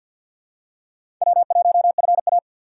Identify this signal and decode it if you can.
Morse code, heard as a keyed tone, about 53 wpm, 700 Hz. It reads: W1FA
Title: USA Call Signs - 4 Characters - 2x Character Spacing - 50wpm